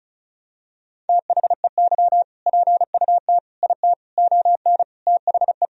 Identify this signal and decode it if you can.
Morse code, heard as a keyed tone, about 36 wpm, 700 Hz. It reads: THEY PUT IT ON THE
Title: Sentences from Top 1000 Words - Review Part 4 - 35wpm